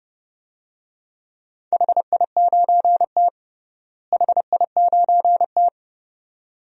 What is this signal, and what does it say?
Morse code, heard as a keyed tone, about 30 wpm, 700 Hz. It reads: HI9T HI9T
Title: DXCC-100 Call Signs - 4 Characters - Repeated 2x - 30wpm